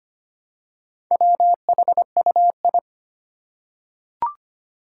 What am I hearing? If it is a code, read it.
Morse code, heard as a keyed tone, about 25 wpm, 700 Hz. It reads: WHUI
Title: Four Challenging Letter-Number Combinations 25wpm